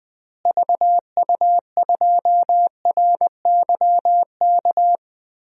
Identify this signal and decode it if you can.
Morse code, heard as a keyed tone, about 20 wpm, 700 Hz. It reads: VU2RYK
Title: DXCC-100 Call Signs - 6 Characters - Repeated 3x - 20wpm